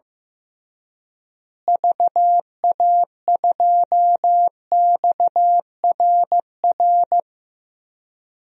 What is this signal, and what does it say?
Morse code, heard as a keyed tone, about 15 wpm, 700 Hz. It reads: VA2XRR
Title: DXCC-100 Call Signs - 6 Characters - Repeated 3x - 15wpm